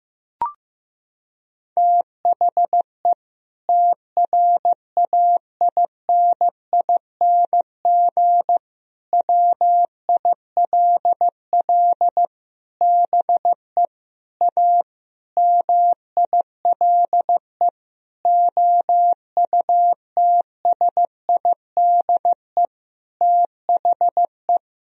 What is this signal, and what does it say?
Morse code, heard as a keyed tone, about 16 wpm, 700 Hz. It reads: THE TRAINING WILL BE A MILE OUTSIDE THE
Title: Sentences from Top 1000 Words - Review Part 4 - 15wpm